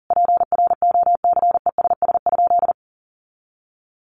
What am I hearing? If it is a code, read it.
Morse code, heard as a keyed tone, about 40 wpm, 700 Hz. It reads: PROCESS?
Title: Sentences from Top 1400 Words 40wpm